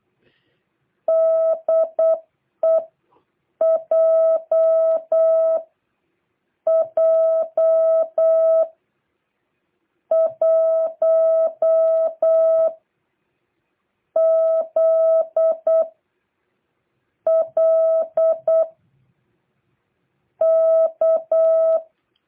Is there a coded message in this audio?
Yes: Morse code from 1.1 to 21.8 s, heard as a keyed tone, about 8 wpm, spaced as for 7 wpm, 650 Hz. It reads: DEJ J 1 Z L K